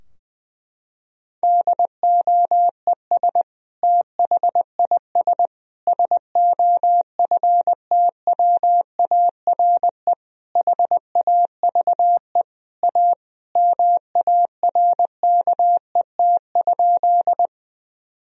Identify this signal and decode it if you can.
Morse code, heard as a keyed tone, about 20 wpm, 700 Hz. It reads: DOES THIS SOFTWARE HAVE A MARKET?
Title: Sentences from Top 1000 Words - Review Part 3 - 20wpm